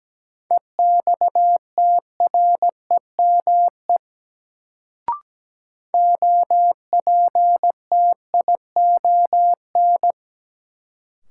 Morse code, heard as a keyed tone, about 17 wpm, 700 Hz. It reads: EXTREME OPTION
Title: Sentences from Top 1300 Words - New Word Drill - 17wpm